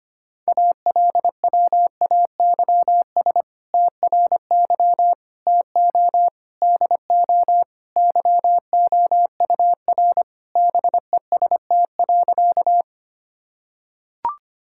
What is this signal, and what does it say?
Morse code, heard as a keyed tone, about 25 wpm, 700 Hz. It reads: ALWAYS TRY TO DO YOUR BEST.
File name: practice / 25wpm / Sentences with 2-Word Phrases - 25wpm